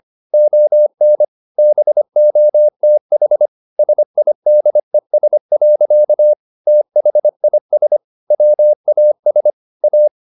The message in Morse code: ON BOTH SIDES. THIS WAS A

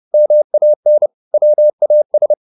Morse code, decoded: MAN WAS

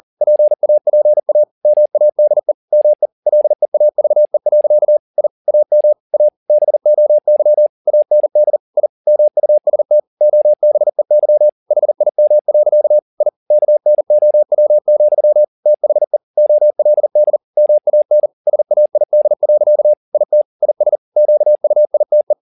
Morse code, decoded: PAPA MADE ME LEAVE. I AM A BOY AND I MUST OBEY HIM. I KNOW, THE OLD MAN SAID. IT IS QUITE

